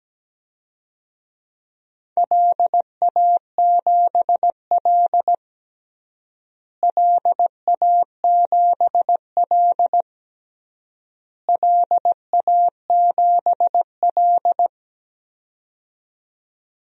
Morse code, heard as a keyed tone, about 17 wpm, 700 Hz. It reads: LA7L LA7L LA7L
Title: DXCC-100 Call Signs - 4 Characters - Repeated 3x - 17wpm